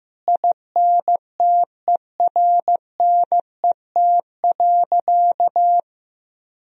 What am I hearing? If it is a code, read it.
Morse code, heard as a keyed tone, about 16 wpm, 700 Hz. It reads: INTERNET.
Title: Sentences from Top 1000 Words - Review Part 4 - 15wpm